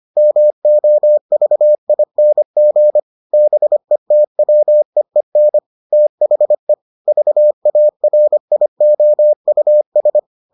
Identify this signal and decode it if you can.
Morse code, heard as a keyed tone, about 25 wpm, 600 Hz. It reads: MOVING BETWEEN THE VARIOUS